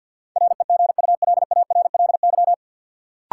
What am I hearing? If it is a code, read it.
Morse code, heard as a keyed tone, about 52 wpm, 700 Hz. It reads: REGULARLY
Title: Sentences from Top 1400 Words - New Word Drill - 50wpm